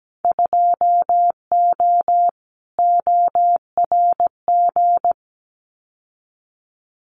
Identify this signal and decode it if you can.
Morse code, heard as a keyed tone, about 17 wpm, 700 Hz. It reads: 2O ORG
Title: Winter Field Day Exchanges 17wpm